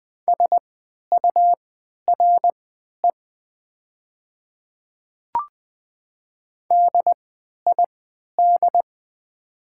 Morse code, heard as a keyed tone, about 20 wpm, spaced as for 11 wpm, 700 Hz. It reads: SURE DID
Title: 3x Character Spacing - Top 300 Words - 20wpm